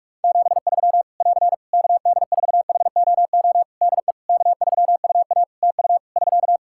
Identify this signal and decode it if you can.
Morse code, heard as a keyed tone, about 46 wpm, 700 Hz. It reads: 73 <AR> KD4HOO DE K3UA TU <SK>
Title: Practice QSO - Exchange 4 - Ending - 45wpm